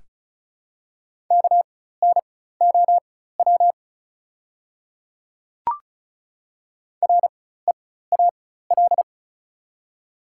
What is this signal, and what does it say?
Morse code, heard as a keyed tone, about 36 wpm, spaced as for 16 wpm, 690 Hz. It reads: KNOW REAL
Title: 4x Character Spacing - Top 300 Words - 35wpm